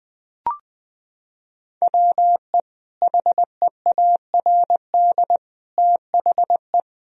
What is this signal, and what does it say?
Morse code, heard as a keyed tone, about 20 wpm, 700 Hz. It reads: WE HEARD THE
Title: Sentences from Top 1000 Words - Review Part 2 - 20wpm